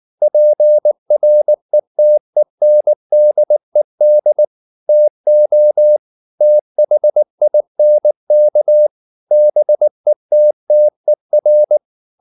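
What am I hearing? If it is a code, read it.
Morse code, heard as a keyed tone, about 19 wpm, 600 Hz. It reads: PRETENDED TO THINK BETTER
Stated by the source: Morse code at 19 wpm from CW med Gnister, reading Ken Follett's novel Never in English.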